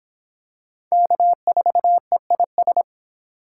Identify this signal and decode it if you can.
Morse code, heard as a keyed tone, about 23 wpm, 700 Hz. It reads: K4EIS